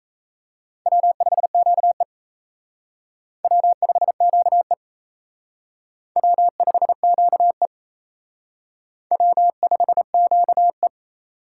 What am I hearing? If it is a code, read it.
Morse code, heard as a keyed tone, about 35 wpm, 700 Hz. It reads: W5QE W5QE W5QE W5QE
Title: Speed Racing - USA 1x2 Call Signs 28wpm